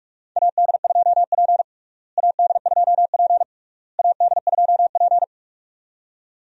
Morse code, heard as a keyed tone, about 46 wpm, 700 Hz. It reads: AD2P AD2P AD2P
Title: USA Call Signs - 4 Characters - Repeated 3x - 45wpm